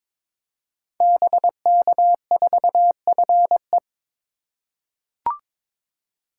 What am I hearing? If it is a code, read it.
Morse code, heard as a keyed tone, about 22 wpm, 700 Hz. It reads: BK4FE
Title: Code Groups - 5 Letter:Numbers - 22wpm